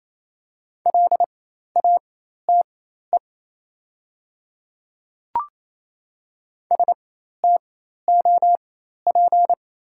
Morse code, heard as a keyed tone, about 28 wpm, spaced as for 13 wpm, 700 Hz. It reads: LATE STOP
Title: 4x Character Spacing - Top 300 Words - 28wpm